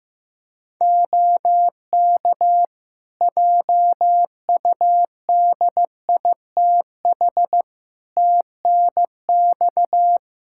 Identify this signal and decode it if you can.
Morse code, heard as a keyed tone, about 15 wpm, 700 Hz. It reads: OK JUDITH TNX
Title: Practice QSO - Exchange 4 - Ending - 15wpm